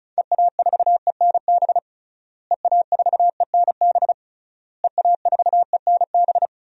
Morse code, heard as a keyed tone, about 36 wpm, 700 Hz. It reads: EA4ENB EA4ENB EA4ENB
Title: DXCC-100 Call Signs - 6 Characters - Repeated 3x - 35wpm